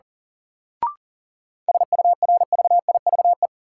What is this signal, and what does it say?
Morse code, heard as a keyed tone, about 37 wpm, 700 Hz. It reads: SURVIVE
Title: Sentences from Top 1300 Words - New Word Drill - 40wpm